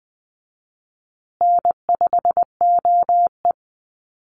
Morse code, heard as a keyed tone, about 19 wpm, 700 Hz. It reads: N5OE
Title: Easy Call Signs Repeated 2x - 20wpm